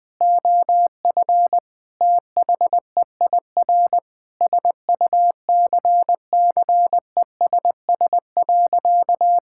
Morse code, heard as a keyed tone, about 20 wpm, 700 Hz. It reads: OF THEIR SUCCESS.
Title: Sentences from Top 1000 Words - Review Part 6 - 20wpm